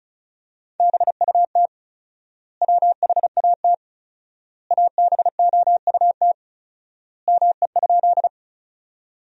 Morse code, heard as a keyed tone, about 36 wpm, 700 Hz. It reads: BUT WHAT ABOUT ME?
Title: 4x Word-Spacing - Sentences from Top 200 Words - 35wpm